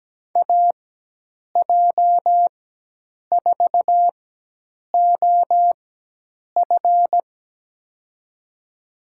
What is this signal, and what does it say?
Morse code, heard as a keyed tone, about 17 wpm, spaced as for 8 wpm, 700 Hz. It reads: AJ4OF
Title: USA Call Signs - 5 Characters - 4x Character Spacing - 17wpm